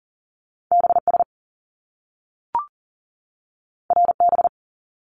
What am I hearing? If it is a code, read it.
Morse code, heard as a keyed tone, about 41 wpm, 700 Hz. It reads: BS RB
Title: Top Two Letter Combinations in Words 40wpm